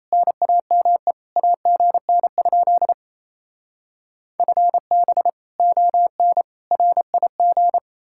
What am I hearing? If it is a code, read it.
Morse code, heard as a keyed tone, about 38 wpm, 700 Hz. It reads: NAME AGN? FB ON RIG